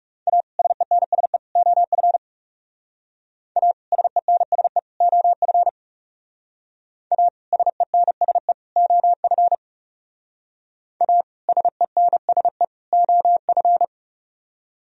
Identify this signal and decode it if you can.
Morse code, heard as a keyed tone, about 37 wpm, 700 Hz. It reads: A SENSE OF A SENSE OF A SENSE OF A SENSE OF
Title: Speed-Racing - Sets of 3 Words 30wpm